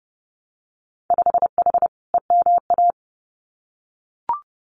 Morse code, heard as a keyed tone, about 32 wpm, 700 Hz. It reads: 5H EMA
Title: Winter Field Day Exchanges Encore 30wpm